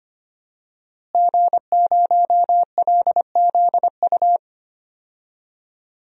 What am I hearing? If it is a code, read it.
Morse code, heard as a keyed tone, about 25 wpm, 700 Hz. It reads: G0LZU
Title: DXCC-100 Call Signs - 5 Characters - Repeated 3x - 25wpm